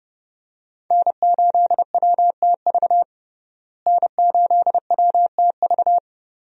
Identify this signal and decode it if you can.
Morse code, heard as a keyed tone, about 30 wpm, 700 Hz. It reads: N8WTV N8WTV
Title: USA Call Signs - 5 Characters - Repeated 2x - 30wpm